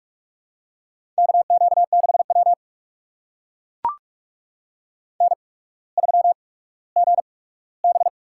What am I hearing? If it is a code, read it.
Morse code, heard as a keyed tone, about 44 wpm, spaced as for 19 wpm, 690 Hz. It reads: KQ6W N3GB
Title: USA Call Signs - 4 Characters - 8x Character Spacing - 45wpm